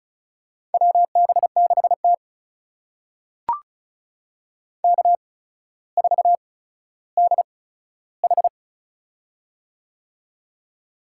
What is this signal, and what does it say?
Morse code, heard as a keyed tone, about 36 wpm, spaced as for 15 wpm, 700 Hz. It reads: WB6T K4DH
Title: USA Call Signs - 4 Characters - 8x Character Spacing - 35wpm